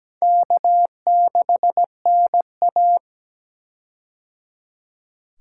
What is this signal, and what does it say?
Morse code, heard as a keyed tone, about 17 wpm, 700 Hz. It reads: K6NA
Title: K1USN SST QSO Practice - 17wpm